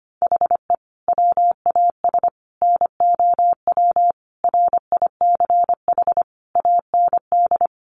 Morse code, heard as a keyed tone, about 25 wpm, 700 Hz. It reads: HE WAS NOW RICH AND
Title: Sentences from Top 1500 Words 25wpm